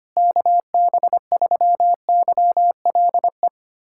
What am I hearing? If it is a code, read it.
Morse code, heard as a keyed tone, about 25 wpm, 700 Hz. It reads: KB3YLE
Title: USA Callsign DE Callsign - 6 Characters - 25wpm